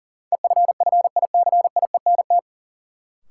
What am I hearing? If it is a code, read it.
Morse code, heard as a keyed tone, about 40 wpm, 700 Hz. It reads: EFFICIENT
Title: Sentences from Top 1300 Words - New Word Drill - 40wpm